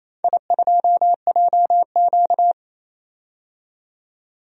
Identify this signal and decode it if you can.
Morse code, heard as a keyed tone, about 28 wpm, 700 Hz. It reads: I2JQ
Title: DXCC-100 Call Signs - 4 Characters - 28wpm